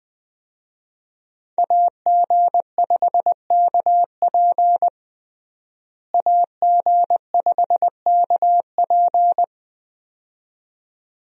Morse code, heard as a keyed tone, about 21 wpm, 700 Hz. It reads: AG5KP AG5KP
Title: USA Call Signs - 5 Characters - Repeated 2x - 20wpm